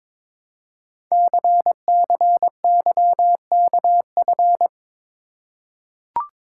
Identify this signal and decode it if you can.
Morse code, heard as a keyed tone, about 22 wpm, 700 Hz. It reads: CCYKF
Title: Code Groups - 5 Letters - 22wpm